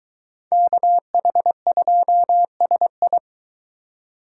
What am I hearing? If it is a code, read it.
Morse code, heard as a keyed tone, about 22 wpm, 700 Hz. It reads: KH2SI